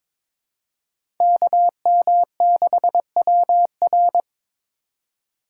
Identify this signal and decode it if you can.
Morse code, heard as a keyed tone, about 22 wpm, 700 Hz. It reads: KM6WR
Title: USA Call Signs - 5 Characters - 2x Character Spacing - 22wpm